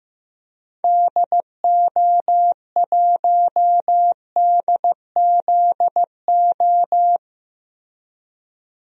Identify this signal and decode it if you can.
Morse code, heard as a keyed tone, about 15 wpm, 700 Hz. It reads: DO1DZO
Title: DXCC-100 Call Signs - 6 Characters - Repeated 2x - 15wpm